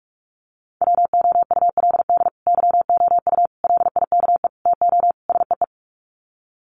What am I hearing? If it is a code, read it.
Morse code, heard as a keyed tone, about 44 wpm, 700 Hz. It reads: WOULD YOU LIKE TO SEE
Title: Sets of 5 Words Encore2 45wpm